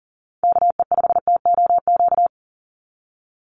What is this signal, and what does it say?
Morse code, heard as a keyed tone, about 40 wpm, 700 Hz. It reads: KE5TOQ
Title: USA Call Signs 40wpm